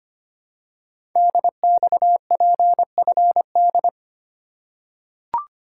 Morse code, heard as a keyed tone, about 25 wpm, 700 Hz. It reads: DXPFD
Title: Code Groups - 5 Letters - 25wpm